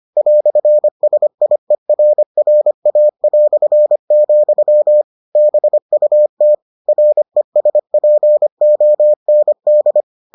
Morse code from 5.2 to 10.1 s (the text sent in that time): BUT RESPOND